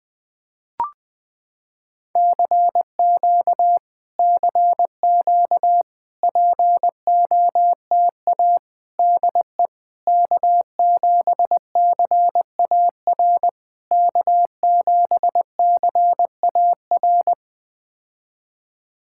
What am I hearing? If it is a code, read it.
Morse code, heard as a keyed tone, about 20 wpm, 700 Hz. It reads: CQ CQ POTA DE K7CAR K7CAR
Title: POTA QSO Practice - 20wpm